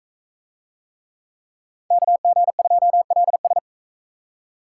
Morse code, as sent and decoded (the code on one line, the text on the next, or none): -.- --. ..--- .-.. ...
KG2LS